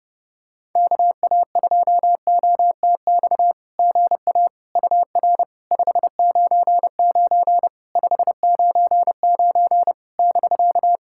Morse code, heard as a keyed tone, about 30 wpm, 700 Hz. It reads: KA2OTX GA UR 599 599 <BK>
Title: SOTA QSO Practice 30wpm